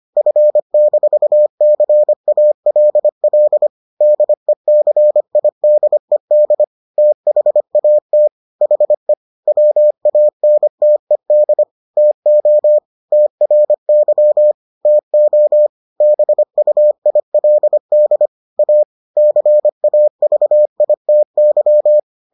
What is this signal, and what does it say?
Morse code, heard as a keyed tone, about 25 wpm, 600 Hz. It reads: F-CALL DECIDED THAT HE WANTED TO TRY TO BUILD A CAVITY